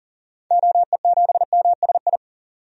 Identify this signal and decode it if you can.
Morse code, heard as a keyed tone, about 40 wpm, 700 Hz. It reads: OE7MSI
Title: DXCC-100 Call Signs - 6 Characters - 40wpm